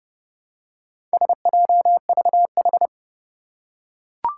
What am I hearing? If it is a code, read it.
Morse code, heard as a keyed tone, about 32 wpm, 700 Hz. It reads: SJVH
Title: Four Challenging Letter-Number Combinations 30wpm